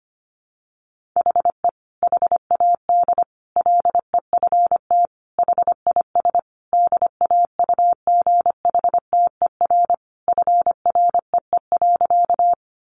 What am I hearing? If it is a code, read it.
Morse code, heard as a keyed tone, about 25 wpm, 700 Hz. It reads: HE HAD LEFT HIS DAUGHTER FREE.
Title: Sentences from Top 1500 Words 25wpm